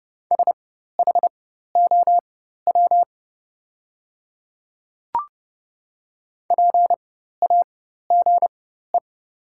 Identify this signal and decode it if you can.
Morse code, heard as a keyed tone, about 30 wpm, spaced as for 14 wpm, 700 Hz. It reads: SHOW PAGE
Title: 4x Character Spacing - Top 500 Words - 30wpm